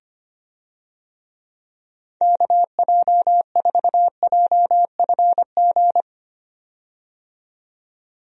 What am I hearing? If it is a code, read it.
Morse code, heard as a keyed tone, about 25 wpm, 700 Hz. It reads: KJ4JFG
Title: USA Call Signs - 6 Characters - 25wpm